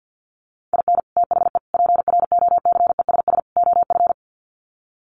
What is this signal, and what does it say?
Morse code, heard as a keyed tone, about 50 wpm, 700 Hz. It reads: IN THE PROCESS OF
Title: Sets of 4 Words 50wpm